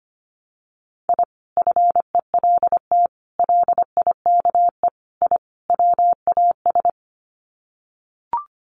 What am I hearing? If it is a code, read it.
Morse code, heard as a keyed tone, about 25 wpm, 700 Hz. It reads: I FELT LIKE I WAS
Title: Sets of 5 Words Encore 25wpm